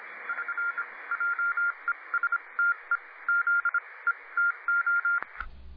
Morse code, heard as a keyed tone, about 26 wpm, spaced as for 21 wpm, 1400 Hz, with radio noise. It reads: <SN>JESTE ZETO